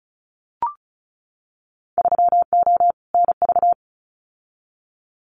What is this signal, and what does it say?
Morse code, heard as a keyed tone, about 34 wpm, 700 Hz. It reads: 3O NV
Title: Winter Field Day Exchanges Encore 35wpm